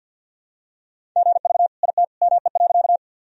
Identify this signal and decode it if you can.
Morse code, heard as a keyed tone, about 47 wpm, 700 Hz. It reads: GV IT ME.
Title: Sentences from Top 200 Words and Abbreviations 50wpm